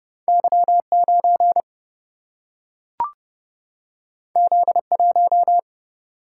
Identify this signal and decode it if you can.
Morse code, heard as a keyed tone, about 31 wpm, 700 Hz. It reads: Y 9 Z 1
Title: Single-Letter Single-Number 30wpm